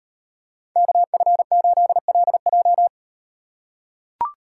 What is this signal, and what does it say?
Morse code, heard as a keyed tone, about 38 wpm, 700 Hz. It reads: KF8LJ